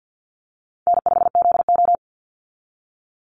Morse code, heard as a keyed tone, about 50 wpm, 700 Hz. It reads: N5ZO
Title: Call Signs - 50wpm